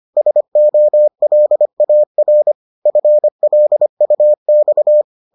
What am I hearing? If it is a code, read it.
Morse code, heard as a keyed tone, about 25 wpm, 600 Hz. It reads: SOLAR FLUX